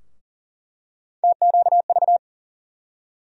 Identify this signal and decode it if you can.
Morse code, heard as a keyed tone, about 40 wpm, 700 Hz. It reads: TQV
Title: USA Call Sign Postfixes - 3 Letters 40wpm